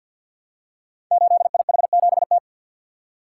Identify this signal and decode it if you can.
Morse code, heard as a keyed tone, about 52 wpm, 700 Hz. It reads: 8IH7T
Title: Code Groups - 5 Letter:Numbers - 50wpm